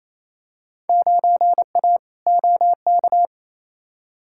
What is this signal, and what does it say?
Morse code, heard as a keyed tone, about 28 wpm, 700 Hz. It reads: 9A OK
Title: Field Day Exchanges Encore - 28wpm